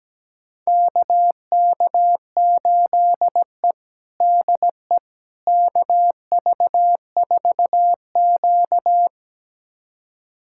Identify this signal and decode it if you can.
Morse code, heard as a keyed tone, about 17 wpm, 700 Hz. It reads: KK8E DE KV4Q
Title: USA Callsign DE Callsign - 4 Characters - 17wpm